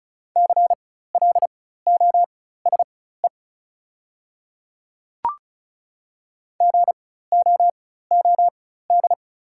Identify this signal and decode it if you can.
Morse code, heard as a keyed tone, about 36 wpm, spaced as for 16 wpm, 700 Hz. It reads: CLOSE GOOD
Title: 4x Character Spacing - Top 300 Words - 35wpm